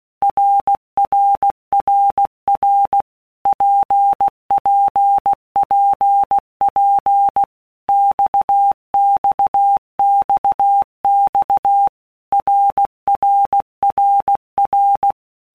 Morse code, heard as a keyed tone, about 16 wpm, 800 Hz. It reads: RRRR PPPP XXXX RRRR
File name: Basic Letter Groups By Blocks / Groups@15 / RKPX@15